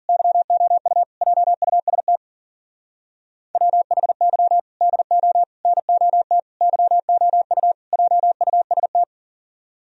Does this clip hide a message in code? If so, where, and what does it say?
Yes: Morse code from 0.1 to 9.0 s, heard as a keyed tone, about 51 wpm, 700 Hz. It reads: YOU JUST WHY DO NOT YOU JUST